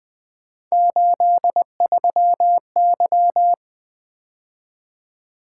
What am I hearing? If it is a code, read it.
Morse code, heard as a keyed tone, about 24 wpm, 700 Hz. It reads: 83Y
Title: Speed-Racing - Three Letter-Number Combinations - 20wpm